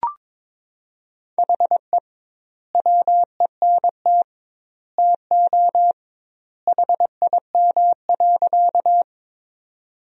Morse code, heard as a keyed tone, about 22 wpm, 700 Hz. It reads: HE WENT TO HIM.
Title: 2x Word-Spacing - Sentences from Top 200 Words - 22wpm